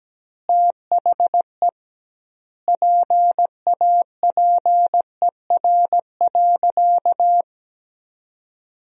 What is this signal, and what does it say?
Morse code, heard as a keyed tone, about 17 wpm, 700 Hz. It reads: THE PAPER.
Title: 2x Word-Spacing - Sentences from Top 300 Words - 17wpm